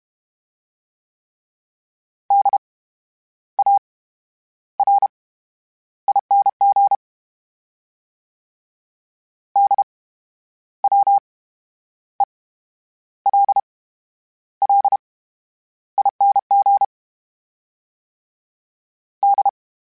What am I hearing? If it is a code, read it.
Morse code, heard as a keyed tone, about 32 wpm, spaced as for 11 wpm, 800 Hz. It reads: DARING DWELLING D
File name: Special INGs / Special INGs-mp3s / VerySlowToFaster Ing @30